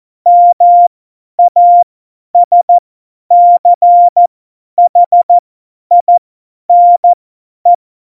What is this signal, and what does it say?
Morse code, heard as a keyed tone, about 14 wpm, spaced as for 10 wpm, 700 Hz. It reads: MASCHINE
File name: Morsekurs / Witze / Witz 05. Januar 2021, 50 F70